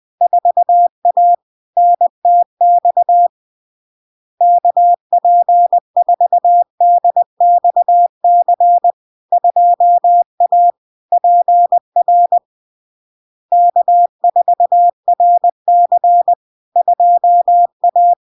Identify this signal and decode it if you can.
Morse code, heard as a keyed tone, about 20 wpm, 700 Hz. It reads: 4A NTX KP4DXC 2A PR K4RC 2A
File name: LARK-FD-2022-Log-in-CW-20wpm-700-Hz